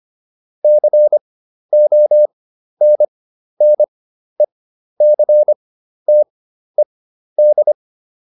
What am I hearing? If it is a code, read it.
Morse code, heard as a keyed tone, about 25 wpm, spaced as for 12 wpm, 600 Hz. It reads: CONNECTED